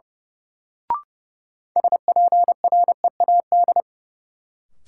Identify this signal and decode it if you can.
Morse code, heard as a keyed tone, about 30 wpm, 700 Hz. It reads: SPREAD
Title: Sentences from Top 1200 Words - New Word Drill Alt - 30wpm